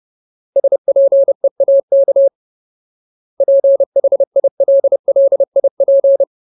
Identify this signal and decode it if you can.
Morse code, heard as a keyed tone, about 30 wpm, 550 Hz. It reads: SPEAK PHILLIP
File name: Ultimate Word List / 30WPM Ultimate